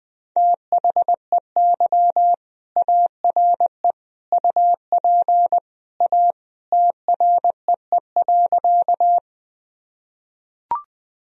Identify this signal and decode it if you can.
Morse code, heard as a keyed tone, about 20 wpm, 700 Hz. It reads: THEY ARE UP A TREE.
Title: Sentences from Top 1000 Words - Review Part 1 - 20wpm